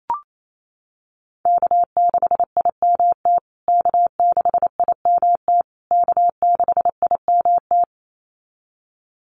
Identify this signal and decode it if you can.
Morse code, heard as a keyed tone, about 28 wpm, 700 Hz. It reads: K6IMT K6IMT K6IMT
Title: USA Call Signs Repeated 3x 28wpm